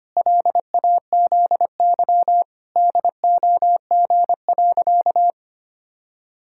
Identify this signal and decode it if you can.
Morse code, heard as a keyed tone, about 25 wpm, 700 Hz. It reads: LAZY DOG.
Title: Pangram Challenge v2 - 25wpm